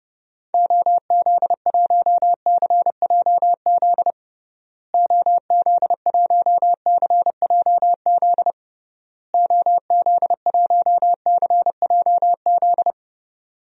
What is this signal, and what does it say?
Morse code, heard as a keyed tone, about 30 wpm, 700 Hz. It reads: OZ1CJZ OZ1CJZ OZ1CJZ
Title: DXCC-100 Call Signs - 6 Characters - Repeated 3x - 30wpm